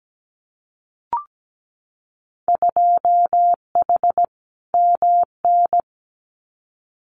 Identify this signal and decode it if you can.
Morse code, heard as a keyed tone, about 17 wpm, 700 Hz. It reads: 2H MN
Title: Winter Field Day Exchanges 17wpm